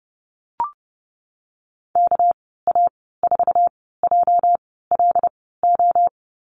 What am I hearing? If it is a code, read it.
Morse code, heard as a keyed tone, about 31 wpm, spaced as for 17 wpm, 700 Hz. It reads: KA4JLO
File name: Lesson 37g - Letter x - Call Signs 30wpm